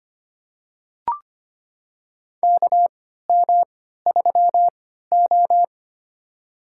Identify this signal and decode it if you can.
Morse code, heard as a keyed tone, about 25 wpm, spaced as for 14 wpm, 700 Hz. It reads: KM3O
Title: USA Call Signs - 4 Characters - 3x Character Spacing - 25wpm